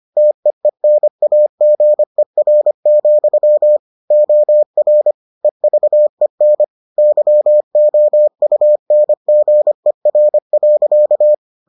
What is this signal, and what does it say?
Morse code, heard as a keyed tone, about 25 wpm, 600 Hz. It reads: TEENAGER, OR EVEN YOUNGER.